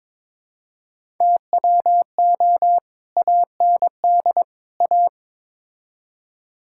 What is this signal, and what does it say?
Morse code, heard as a keyed tone, about 22 wpm, 700 Hz. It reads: TWO AND A
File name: practice / 22wpm / Build A Phrase - 5-Word Phrases - 22wpm